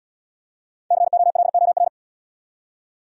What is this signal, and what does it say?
Morse code, heard as a keyed tone, about 65 wpm, 690 Hz. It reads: COLOR